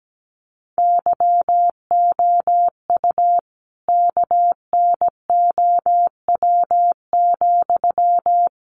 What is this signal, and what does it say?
Morse code, heard as a keyed tone, about 17 wpm, 700 Hz. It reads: YOU KNOW,
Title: Sentences from Top 200 Words 17wpm